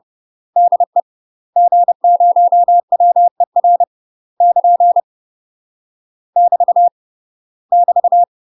Morse code, heard as a keyed tone, about 30 wpm, 700 Hz. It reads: DE G0WER <KN> <BT> <BT>
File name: UK MASTER QSOs TRAINING / 30WPM / UK MASTER QSOs 0049